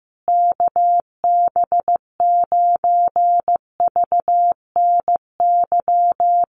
Morse code, heard as a keyed tone, about 15 wpm, 700 Hz. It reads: KB9VNY
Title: USA Call Signs Repeated 3x 15wpm